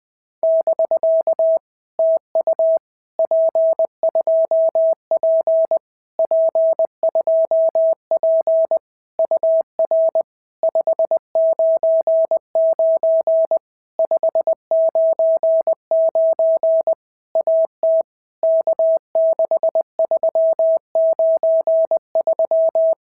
Morse code, heard as a keyed tone, about 20 wpm, 650 Hz. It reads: <BK> TU P2P P2P UR 599 599 AT K6393
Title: POTA QSO Practice - 20wpm